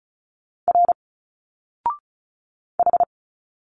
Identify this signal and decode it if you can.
Morse code, heard as a keyed tone, about 34 wpm, 700 Hz. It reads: R H
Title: Single Letters 35wpm